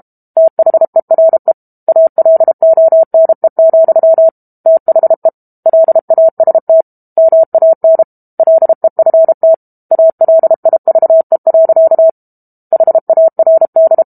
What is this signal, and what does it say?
Morse code, heard as a keyed tone, about 32 wpm, 650 Hz. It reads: THERE ALONE, THE LAST MAN LEFT ALIVE. HARD
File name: Book Two / 06-Chapter 6 - 32 WPM